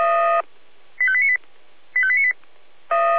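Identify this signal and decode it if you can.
Five-tone selective call (ZVEI), heard as quick runs of five tones, tones of about 70 ms. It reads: 85898 85898
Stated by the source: ZVEI five-tone selective call received by SDR, from the feuerwehr-orlamuende.org audio archive.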